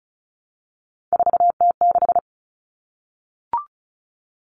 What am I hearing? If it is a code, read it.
Morse code, heard as a keyed tone, about 34 wpm, 700 Hz. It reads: 4T6
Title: Common Cut Numbers - 3-Digit 35wpm